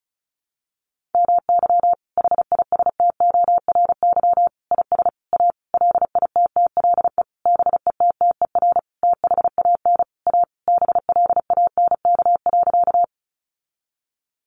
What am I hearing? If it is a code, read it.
Morse code, heard as a keyed tone, about 34 wpm, 700 Hz. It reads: MY HISTORY IS A LITTLE BETTER THAN A BLANK.
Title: Sentences from Top 1500 Words 35wpm